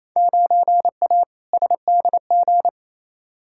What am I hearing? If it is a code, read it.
Morse code, heard as a keyed tone, about 28 wpm, 700 Hz. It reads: 9A SDG
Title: Field Day Exchanges Encore - 28wpm